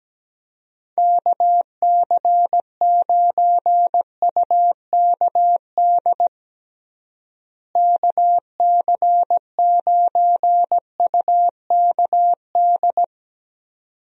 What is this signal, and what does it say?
Morse code, heard as a keyed tone, about 17 wpm, 700 Hz. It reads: KC9UKD KC9UKD
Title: USA Call Signs - 6 Characters - Repeated 2x - 17wpm